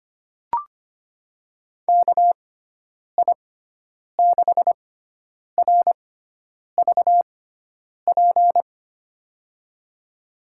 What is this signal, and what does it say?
Morse code, heard as a keyed tone, about 25 wpm, spaced as for 9 wpm, 700 Hz. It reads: KI6RVP
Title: USA Call Signs - 6 Characters - 6x Character Spacing - 25wpm